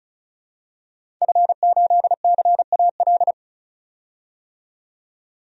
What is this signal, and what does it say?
Morse code, heard as a keyed tone, about 34 wpm, 700 Hz. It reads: F8CAL
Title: DXCC-100 Call Signs - 5 Characters - 35wpm